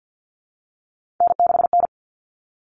Morse code, heard as a keyed tone, about 50 wpm, 700 Hz. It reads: N6N
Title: Common Cut Numbers - 3-Digit 50wpm